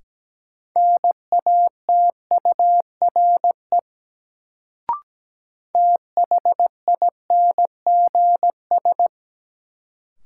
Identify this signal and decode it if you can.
Morse code, heard as a keyed tone, about 17 wpm, 700 Hz. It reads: NATURE THINGS
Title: Sets of 3 Words - New Word Drill - 17wpm